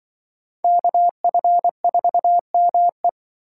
Morse code, heard as a keyed tone, about 24 wpm, 700 Hz. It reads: KF4ME
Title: Speed Racing - USA 2x2 Call Signs 20wpm